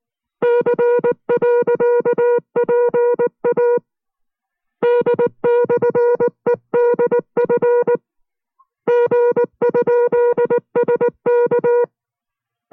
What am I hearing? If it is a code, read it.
Morse code, heard as a keyed tone, about 19 wpm, 480 Hz. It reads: C.PA D/EDF G?SK